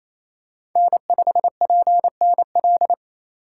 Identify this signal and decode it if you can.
Morse code, heard as a keyed tone, about 28 wpm, 700 Hz. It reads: N5PNL
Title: USA Call Signs - 5 Characters - 6x Character Spacing - 28wpm